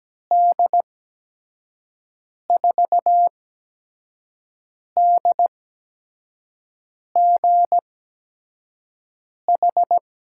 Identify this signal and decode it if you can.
Morse code, heard as a keyed tone, about 17 wpm, 700 Hz. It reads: D 4 D G H
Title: USA Call Signs - 6 Characters - 8x Character Spacing - 17wpm